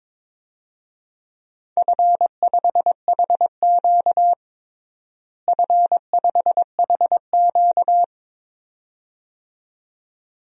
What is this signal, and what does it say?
Morse code, heard as a keyed tone, about 22 wpm, 700 Hz. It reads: F5HQ F5HQ
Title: DXCC-100 Call Signs - 4 Characters - Repeated 2x - 22wpm